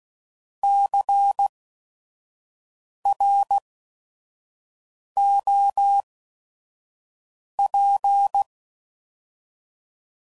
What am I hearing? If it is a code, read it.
Morse code, heard as a keyed tone, about 16 wpm, spaced as for 5 wpm, 800 Hz. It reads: CROP
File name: Prefixes / By First Letter / Cs